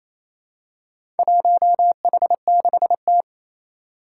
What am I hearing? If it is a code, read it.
Morse code, heard as a keyed tone, about 27 wpm, 700 Hz. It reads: 1H6T
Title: Four Challenging Letter-Number Combinations 28wpm